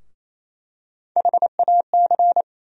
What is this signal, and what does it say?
Morse code, heard as a keyed tone, about 28 wpm, 700 Hz. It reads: HAC
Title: USA Call Sign Postfixes - 3 Letters 28wpm